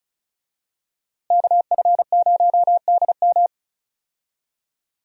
Morse code, heard as a keyed tone, about 36 wpm, 700 Hz. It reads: KF0DM
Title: USA Call Signs - 5 Characters - Repeated 2x - 35wpm